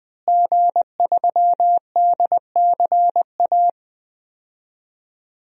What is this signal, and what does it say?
Morse code, heard as a keyed tone, about 21 wpm, 700 Hz. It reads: G3DCA
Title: DXCC-100 Call Signs - 5 Characters - Repeated 3x - 20wpm